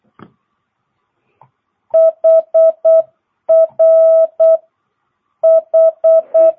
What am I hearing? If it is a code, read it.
Morse code, heard as a keyed tone, about 8 wpm, spaced as for 7 wpm, 650 Hz. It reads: HRH
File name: jh8cbh / 2022cw